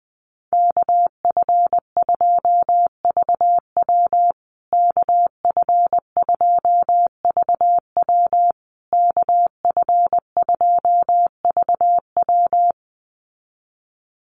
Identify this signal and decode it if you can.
Morse code, heard as a keyed tone, about 20 wpm, 700 Hz. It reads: KF2VW KF2VW KF2VW
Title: USA Call Signs Repeated 3x 20wpm